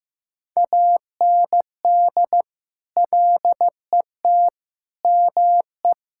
Morse code, heard as a keyed tone, about 16 wpm, 700 Hz. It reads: AND LET ME
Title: Sentences from Top 1600 Words 15wpm